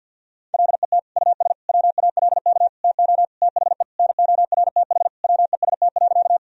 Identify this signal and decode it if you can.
Morse code, heard as a keyed tone, about 50 wpm, 700 Hz. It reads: LET US WALK TO THE NORTH WEST.
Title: Sentences from Top 1000 Words - Review Part 2 - 50wpm